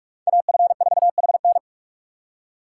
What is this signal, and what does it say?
Morse code, heard as a keyed tone, about 58 wpm, 700 Hz. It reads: AF4HN